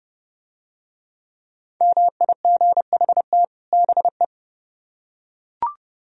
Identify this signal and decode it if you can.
Morse code, heard as a keyed tone, about 29 wpm, 700 Hz. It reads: MIGHT BE